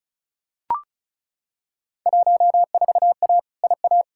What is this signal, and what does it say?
Morse code, heard as a keyed tone, about 36 wpm, 700 Hz. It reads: 14A IA
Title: Field Day Exchanges Encore - 35wpm